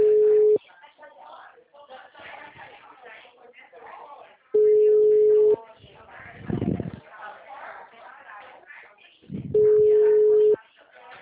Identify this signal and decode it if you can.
Ringback tone (Europe, 425 Hz), with speech.
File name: out- -2016- - - .83333